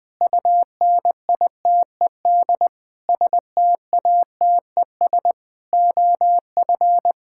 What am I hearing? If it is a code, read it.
Morse code, heard as a keyed tone, about 21 wpm, 700 Hz. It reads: UNITED STATES OF
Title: Sentences with 5-Word Phrases - 20wpm